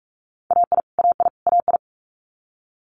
Morse code, heard as a keyed tone, about 46 wpm, 700 Hz. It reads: AI AI AI
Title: Multi-letter Phonemes Repeated 3x 45wpm